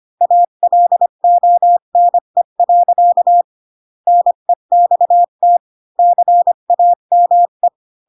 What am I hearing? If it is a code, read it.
Morse code, heard as a keyed tone, about 25 wpm, 700 Hz. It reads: ALONE. NEXT CAME
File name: Pinocchio / Pinocchio ch3 20 25 ik1hge